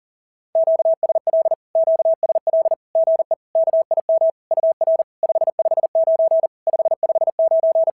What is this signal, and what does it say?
Morse code, heard as a keyed tone, about 37 wpm, 650 Hz. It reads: QSL QSL GE KIM UR 559 559
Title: SOTA QSO Practice 40wpm